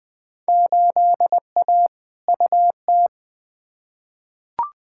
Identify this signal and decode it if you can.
Morse code, heard as a keyed tone, about 21 wpm, 700 Hz. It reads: 8A UT
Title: Field Day Exchanges Encore - 20wpm